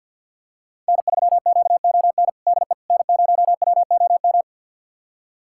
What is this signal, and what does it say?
Morse code, heard as a keyed tone, about 52 wpm, 700 Hz. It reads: N3QON DE N0WOM
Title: USA Callsign DE Callsign - 5 Characters - 50wpm